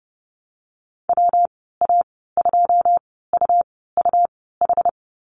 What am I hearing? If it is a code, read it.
Morse code, heard as a keyed tone, about 30 wpm, spaced as for 17 wpm, 700 Hz. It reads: WA2UUH
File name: Lesson 19g - Letter w - Call Signs 30wpm